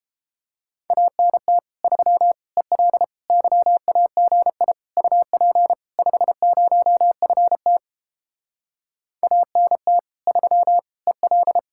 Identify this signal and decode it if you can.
Morse code, heard as a keyed tone, about 29 wpm, 700 Hz. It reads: ANT 3 EL YAGI UP 50FT ANT 3 EL